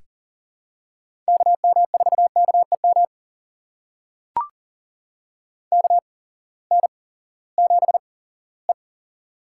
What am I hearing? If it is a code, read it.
Morse code, heard as a keyed tone, about 38 wpm, spaced as for 20 wpm, 700 Hz. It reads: KM4KEM KN7E
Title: USA Call Signs - 8x Character Spacing - 40wpm